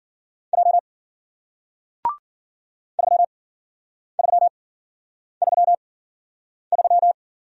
Morse code, heard as a keyed tone, about 49 wpm, spaced as for 9 wpm, 700 Hz. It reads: J 3333
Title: Speed-Racing Single Letter-Number 40wpm